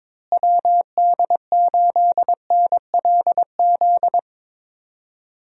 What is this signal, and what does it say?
Morse code, heard as a keyed tone, about 22 wpm, 700 Hz. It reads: WD8NLZ
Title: USA Call Signs - 6 Characters - Repeated 2x - 22wpm